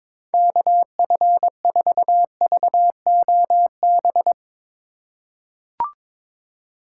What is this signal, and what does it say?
Morse code, heard as a keyed tone, about 22 wpm, 700 Hz. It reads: KF4VOB
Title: USA Call Signs - 4x Character Spacing - 22wpm